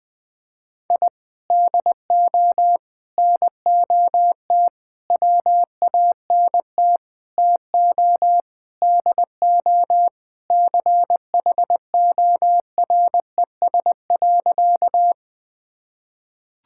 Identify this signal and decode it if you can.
Morse code, heard as a keyed tone, about 20 wpm, 700 Hz. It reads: I DO NOT WANT TO DO CHORES.
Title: Sentences with 5-Word Phrases - Spoken Beginning - 20wpm